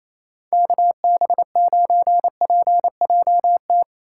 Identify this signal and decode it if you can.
Morse code, heard as a keyed tone, about 28 wpm, 700 Hz. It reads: KB9PJT
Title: USA Call Signs - 2x Character Spacing - 28wpm